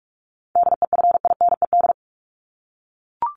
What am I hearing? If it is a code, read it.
Morse code, heard as a keyed tone, about 44 wpm, 700 Hz. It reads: DEFINED
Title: ED Suffix - Top 100 Words with Base Words 45wpm